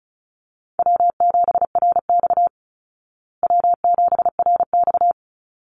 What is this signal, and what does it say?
Morse code, heard as a keyed tone, about 34 wpm, 700 Hz. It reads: W7RX W7RX
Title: Easy Call Signs Repeated 2x - 35wpm